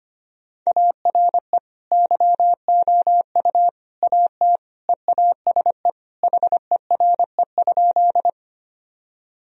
Morse code, heard as a keyed tone, about 25 wpm, 700 Hz. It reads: ARE YOU AT EASE HERE?
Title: Sentences from Top 1000 Words - Review Part 1 - 25wpm